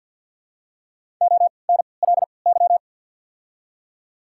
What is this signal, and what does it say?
Morse code, heard as a keyed tone, about 53 wpm, spaced as for 29 wpm, 690 Hz. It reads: ONLY
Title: 3x Character Spacing - Top 300 Words - 50wpm